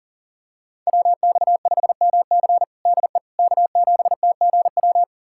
Morse code, heard as a keyed tone, about 40 wpm, 700 Hz. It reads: WX5MC DE K7TGW
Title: USA Callsign DE Callsign - 5 Characters - 40wpm